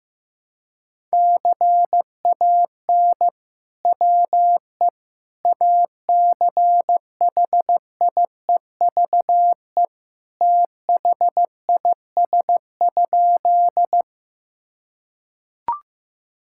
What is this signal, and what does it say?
Morse code, heard as a keyed tone, about 15 wpm, 700 Hz. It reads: CAN WE ACHIEVE THIS?
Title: Sentences from Top 1000 Words - Review Part 6 - 15wpm